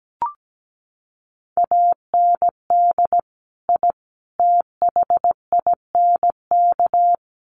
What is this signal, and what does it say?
Morse code, heard as a keyed tone, about 17 wpm, 700 Hz. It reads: AND I THINK
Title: Sets of 3 Words 17wpm